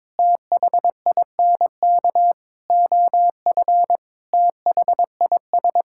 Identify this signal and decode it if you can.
Morse code, heard as a keyed tone, about 22 wpm, 700 Hz. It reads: THINK OF THIS
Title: Sentences from Top 1600 Words 22wpm